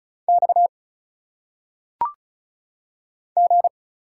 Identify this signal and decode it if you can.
Morse code, heard as a keyed tone, about 36 wpm, 700 Hz. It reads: X G
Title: Single Letters - Phonetic Pronunciation 35wpm